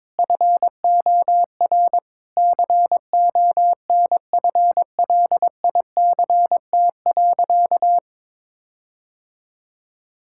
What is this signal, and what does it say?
Morse code, heard as a keyed tone, about 22 wpm, 700 Hz. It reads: FOR CONFLICT.
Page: morse-code-ninja-practice-22wpm